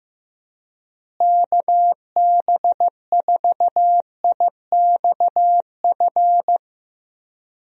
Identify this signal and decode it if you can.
Morse code, heard as a keyed tone, about 15 wpm, 700 Hz. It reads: KB4IXF
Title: USA Call Signs - 6 Characters - Repeated 3x - 15wpm